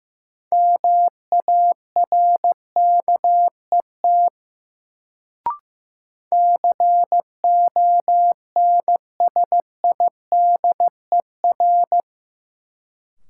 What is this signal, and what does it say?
Morse code, heard as a keyed tone, about 15 wpm, 700 Hz. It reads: MARKET CONSIDER
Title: Sentences from Top 600 Words - New Word Drill - 15wpm